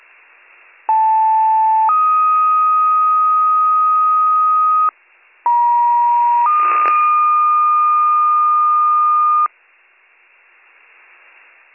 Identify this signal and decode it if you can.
Two-tone page, heard as a one-second tone, then a three-second tone, about 880 then 1250 Hz, about 950 then 1220 Hz. It is heard from 0.9 to 9.5 s.